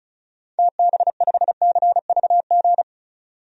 Morse code, heard as a keyed tone, about 34 wpm, 700 Hz. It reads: TB5CVG
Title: DXCC-100 Call Signs - 6 Characters - 35wpm